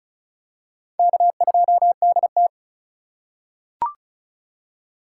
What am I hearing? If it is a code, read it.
Morse code, heard as a keyed tone, about 36 wpm, 700 Hz. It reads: K2DT
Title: USA Call Signs - 4 Characters - 3x Character Spacing - 35wpm